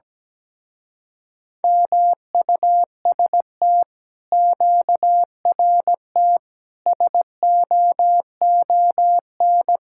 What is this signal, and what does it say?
Morse code, heard as a keyed tone, about 17 wpm, 700 Hz. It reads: MUST QRT SOON
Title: Speed-Racing Common QSO Phrases - 17wpm